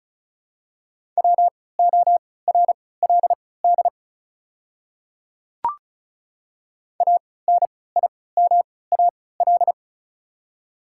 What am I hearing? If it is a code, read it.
Morse code, heard as a keyed tone, about 36 wpm, spaced as for 20 wpm, 700 Hz. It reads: WORLD ANIMAL
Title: 3x Character Spacing - Top 200 Words - 35wpm